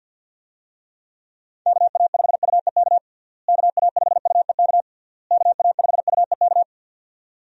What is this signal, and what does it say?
Morse code, heard as a keyed tone, about 52 wpm, 700 Hz. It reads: KA5UEK KA5UEK KA5UEK
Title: USA Call Signs - 6 Characters - Repeated 3x - 50wpm